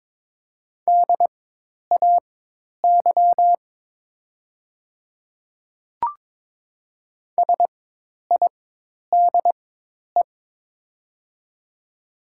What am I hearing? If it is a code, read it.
Morse code, heard as a keyed tone, about 22 wpm, spaced as for 10 wpm, 700 Hz. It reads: DAY SIDE
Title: 4x Character Spacing - Top 100 Words - 22wpm